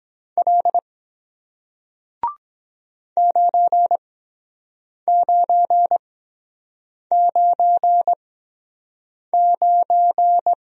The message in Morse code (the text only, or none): L 9999